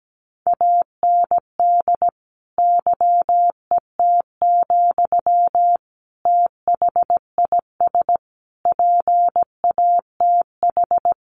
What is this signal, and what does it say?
Morse code, heard as a keyed tone, about 17 wpm, 700 Hz. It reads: AND YET, THIS PATH
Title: Sentences from Top 1400 Words 17wpm